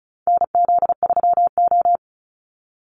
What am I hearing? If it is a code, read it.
Morse code, heard as a keyed tone, about 34 wpm, 700 Hz. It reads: NZ3O
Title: Call Signs - 35wpm